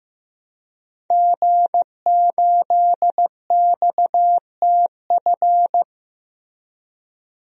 Morse code, heard as a keyed tone, about 15 wpm, 700 Hz. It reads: G8XTF
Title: DXCC-100 Call Signs - 5 Characters - Repeated 3x - 15wpm